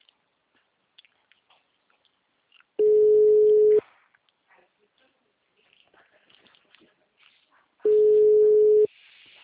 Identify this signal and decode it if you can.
Ringback tone (Europe, 425 Hz).